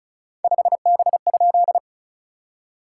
Morse code, heard as a keyed tone, about 36 wpm, 690 Hz. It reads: 5B?
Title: SOTA QSO Practice 35wpm